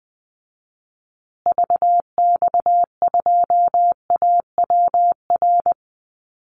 Morse code, heard as a keyed tone, about 20 wpm, 700 Hz. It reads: VX2AWR
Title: Call Signs Repeated 2x - 20wpm